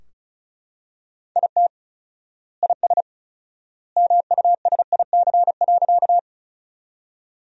Morse code, heard as a keyed tone, about 36 wpm, 700 Hz. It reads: IT IS MUSIC.
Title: 4x Word-Spacing - Sentences from Top 300 Words - 35wpm